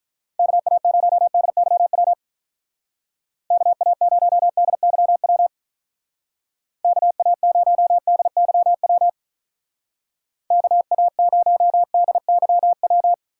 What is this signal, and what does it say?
Morse code, heard as a keyed tone, about 44 wpm, 700 Hz. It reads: KA0DYW KA0DYW KA0DYW KA0DYW